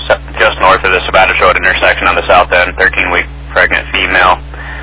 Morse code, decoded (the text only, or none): none